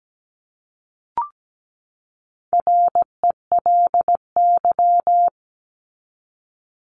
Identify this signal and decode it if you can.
Morse code, heard as a keyed tone, about 17 wpm, 700 Hz. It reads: RELY
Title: Top 1001-1250 Words 17wpm